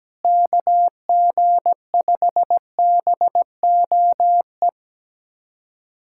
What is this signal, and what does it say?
Morse code, heard as a keyed tone, about 17 wpm, 700 Hz. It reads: KG5BOE
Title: USA Callsign DE Callsign - 6 Characters - 17wpm